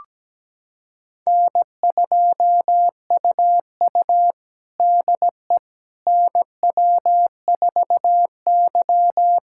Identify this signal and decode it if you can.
Morse code, heard as a keyed tone, about 17 wpm, 700 Hz. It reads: N2UU DE NW4Y
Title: USA Callsign DE Callsign - 4 Characters - 17wpm